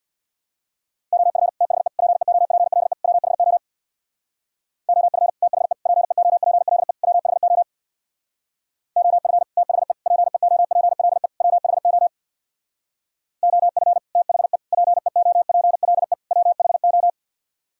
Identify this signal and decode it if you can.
Morse code, heard as a keyed tone, about 64 wpm, 700 Hz. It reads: OF THE PEOPLE WHO OF THE PEOPLE WHO OF THE PEOPLE WHO OF THE PEOPLE WHO